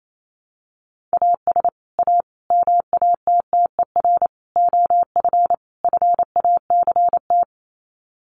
Morse code, heard as a keyed tone, about 28 wpm, 700 Hz. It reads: AS A MATTER OF FACT
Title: Sets of 5 Words 28wpm